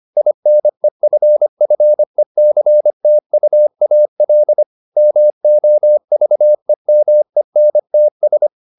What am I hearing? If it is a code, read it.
Morse code, heard as a keyed tone, about 25 wpm, 600 Hz. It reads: INEFFECTUAL MOVEMENTS
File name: the old man and the sea part 1 25wpm